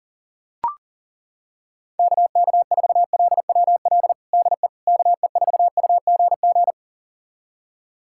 Morse code, heard as a keyed tone, about 40 wpm, 700 Hz. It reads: KK4LWL DE KE4UGG
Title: USA Callsign DE Callsign - 6 Characters - 40wpm